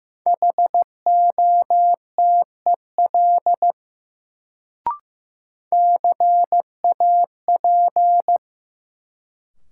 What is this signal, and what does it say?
Morse code, heard as a keyed tone, about 15 wpm, 700 Hz. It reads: HOTEL CAP
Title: Sentences from Top 1600 Words - New Word Drill Alt - 15wpm